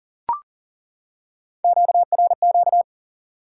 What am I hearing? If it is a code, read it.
Morse code, heard as a keyed tone, about 37 wpm, 700 Hz. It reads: QRQ
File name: IQE Course 3 - Lesson 89 - YAGI - Two Elements - 40wpm